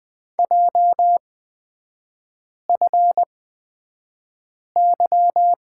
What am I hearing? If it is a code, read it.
Morse code, heard as a keyed tone, about 20 wpm, spaced as for 5 wpm, 700 Hz. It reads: JFY
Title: 224 Lesson 36e - BT - Three Characters 20wpm